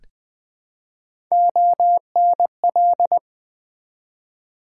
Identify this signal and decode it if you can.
Morse code, heard as a keyed tone, about 20 wpm, 700 Hz. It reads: ONL
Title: Build A Word - 1 Letter At A Time - 4-Letter Words - 20wpm